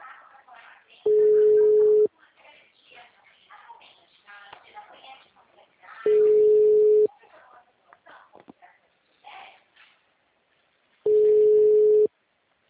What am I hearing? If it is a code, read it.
Ringback tone (Europe, 425 Hz), with speech.